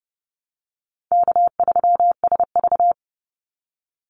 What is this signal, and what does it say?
Morse code, heard as a keyed tone, about 30 wpm, 700 Hz. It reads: K3SV
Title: Easy Call Signs - 30wpm